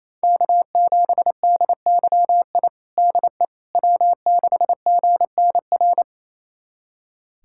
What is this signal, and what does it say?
Morse code, heard as a keyed tone, about 28 wpm, 700 Hz. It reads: K7DYI DE W6GNR
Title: USA Callsign DE Callsign - 5 Characters - 28wpm